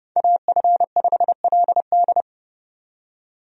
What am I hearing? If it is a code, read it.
Morse code, heard as a keyed tone, about 30 wpm, 700 Hz. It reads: AF5LD
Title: USA Call Signs - 5 Characters - 30wpm